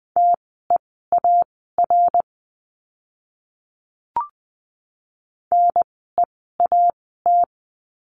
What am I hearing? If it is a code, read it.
Morse code, heard as a keyed tone, about 20 wpm, spaced as for 14 wpm, 700 Hz. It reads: TEAR NEAT
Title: 061 Lesson 11f - Letter h - Words 20wpm